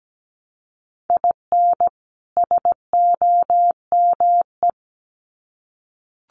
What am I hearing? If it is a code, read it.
Morse code, heard as a keyed tone, about 17 wpm, 700 Hz. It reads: IN SOME
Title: Sets of 2 Words 17wpm